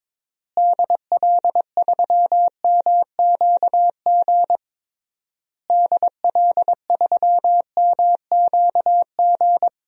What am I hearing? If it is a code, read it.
Morse code, heard as a keyed tone, about 22 wpm, 700 Hz. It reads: DL3MQG DL3MQG
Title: DXCC-100 Call Signs - 6 Characters - Repeated 3x - 22wpm